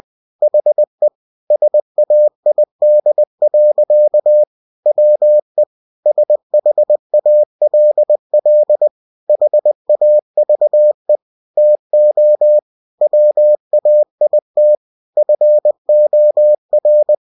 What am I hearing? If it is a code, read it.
Morse code, heard as a keyed tone, about 20 wpm, 600 Hz. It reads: HE SAID. WE SHALL HAVE TO WAIT FOR